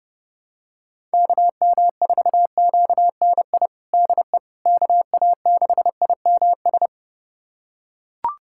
Morse code, heard as a keyed tone, about 30 wpm, 700 Hz. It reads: KM4QNI DE KA6IMS
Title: USA Callsign DE Callsign - 6 Characters - 30wpm